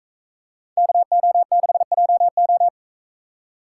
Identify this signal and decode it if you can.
Morse code, heard as a keyed tone, about 40 wpm, 700 Hz. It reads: KO6JO